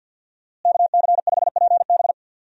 Morse code, heard as a keyed tone, about 51 wpm, 700 Hz. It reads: KC5PB
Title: USA Call Signs - 5 Characters - Repeated 3x - 50wpm